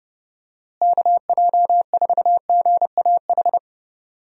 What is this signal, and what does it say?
Morse code, heard as a keyed tone, about 32 wpm, 700 Hz. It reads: KJ4GAH
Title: USA Call Signs - 6 Characters - Repeated 3x - 30wpm